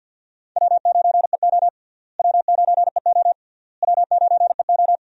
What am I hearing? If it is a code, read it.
Morse code, heard as a keyed tone, about 53 wpm, 700 Hz. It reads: W9EO W9EO W9EO
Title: USA Call Signs - 4 Characters - Repeated 3x - 50wpm